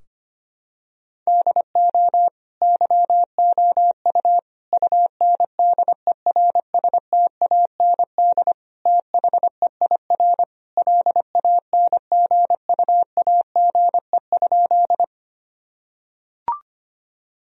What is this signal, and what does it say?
Morse code, heard as a keyed tone, about 25 wpm, 700 Hz. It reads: DO YOU UNDERSTAND THEIR LANGUAGE?
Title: Sentences from Top 1000 Words - Review Part 3 - 25wpm